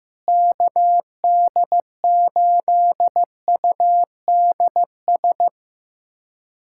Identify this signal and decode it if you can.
Morse code, heard as a keyed tone, about 16 wpm, 700 Hz. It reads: KD8UDS
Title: USA Callsign DE Callsign - 6 Characters - 15wpm